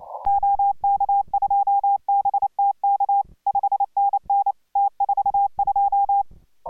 Morse code, heard as a keyed tone, about 29 wpm, 800 Hz. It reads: OK2BTK 5NN T42